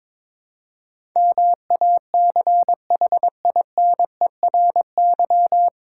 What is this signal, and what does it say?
Morse code, heard as a keyed tone, about 22 wpm, 700 Hz. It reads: MACHINERY
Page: Morse Code Ninja practice 22wpm